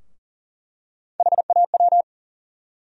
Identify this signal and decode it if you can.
Morse code, heard as a keyed tone, about 40 wpm, 700 Hz. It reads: HAW
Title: USA Call Sign Postfixes - 3 Letters 40wpm